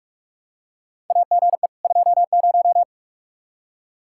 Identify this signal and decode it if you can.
Morse code, heard as a keyed tone, about 46 wpm, 700 Hz. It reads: AGE 20
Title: Practice QSO - Exchange 3 - More Info - 45wpm